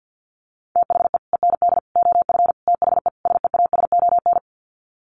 Morse code, heard as a keyed tone, about 50 wpm, 700 Hz. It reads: THE END OF THE SEASON
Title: Sets of 5 Words Encore 50wpm